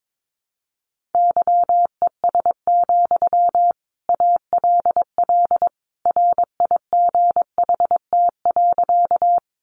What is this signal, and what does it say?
Morse code, heard as a keyed tone, about 22 wpm, 700 Hz. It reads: YES, ALL RIGHT.
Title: Sentences from Top 500 Words 22wpm